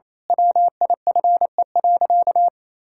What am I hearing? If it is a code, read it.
Morse code, heard as a keyed tone, about 28 wpm, 700 Hz. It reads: WIFE.
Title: Pangram Challenge v2 - 28wpm